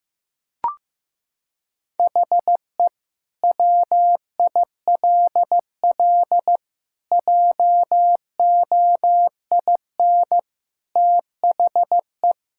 Morse code, heard as a keyed tone, about 15 wpm, 700 Hz. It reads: HE WILL JOIN THE
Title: Sentences from Top 1000 Words - Review Part 6 - 15wpm